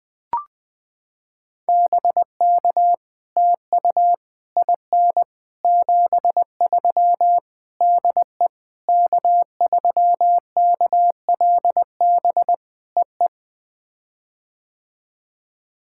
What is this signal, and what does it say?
Morse code, heard as a keyed tone, about 20 wpm, 700 Hz. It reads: BK TU IN 73 DE K3KLB EE
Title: POTA QSO Practice - 20wpm